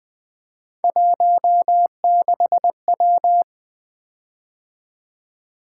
Morse code, heard as a keyed tone, about 20 wpm, 700 Hz. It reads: 16W
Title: Three Letter-Number Combinations - 20wpm